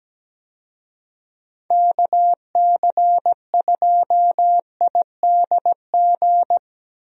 Morse code, heard as a keyed tone, about 17 wpm, 700 Hz. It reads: KC2IDG
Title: USA Call Signs - 6 Characters - Repeated 2x - 17wpm